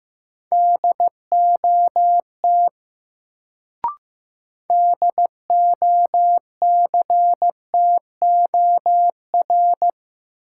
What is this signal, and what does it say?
Morse code, heard as a keyed tone, about 15 wpm, 700 Hz. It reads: DOT DOCTOR
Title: Sentences from Top 1500 Words - New Word Drill - 15wpm